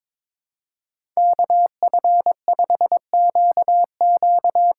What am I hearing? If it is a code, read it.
Morse code, heard as a keyed tone, about 22 wpm, 700 Hz. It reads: KF5QQ
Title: USA Call Signs - 8x Character Spacing - 22wpm